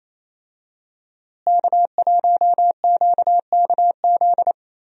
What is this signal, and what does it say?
Morse code, heard as a keyed tone, about 28 wpm, 700 Hz. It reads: K1QKZ
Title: USA Call Signs - 5 Characters - Repeated 2x - 28wpm